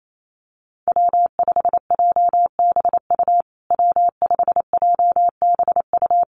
Morse code, heard as a keyed tone, about 28 wpm, 700 Hz. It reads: W5JBU W5JBU
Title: USA Call Signs Repeated 3x 28wpm